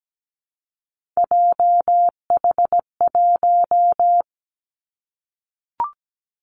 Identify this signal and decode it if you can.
Morse code, heard as a keyed tone, about 17 wpm, 700 Hz. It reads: JH1
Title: Callsign Prefixes for Top 25 DXCC - 17wpm